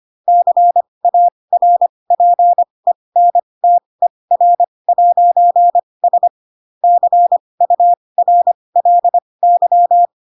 Morse code, heard as a keyed tone, about 25 wpm, spaced as for 20 wpm, 700 Hz. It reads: CARPENTER'S CURLY
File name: Pinocchio / Pinocchio ch2 15 25 ik1hge